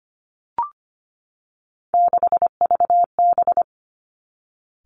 Morse code, heard as a keyed tone, about 25 wpm, 700 Hz. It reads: 6VB
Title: Three Challenging Letter-Number Combinations 25wpm